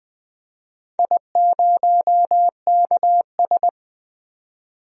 Morse code, heard as a keyed tone, about 20 wpm, 700 Hz. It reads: I0KS
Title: DXCC-100 Call Signs - 4 Characters - 20wpm